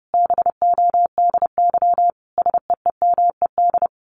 Morse code, heard as a keyed tone, about 30 wpm, 700 Hz. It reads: BODY SEEMED
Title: A Sound Of Thunder 30wpm